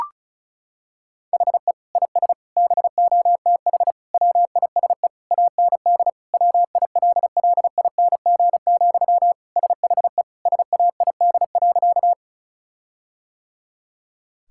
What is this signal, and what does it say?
Morse code, heard as a keyed tone, about 36 wpm, 700 Hz. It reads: HE IS BOTH WISE AND WILLING, SHE SAID.
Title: Sentences from Top 1600 Words 35wpm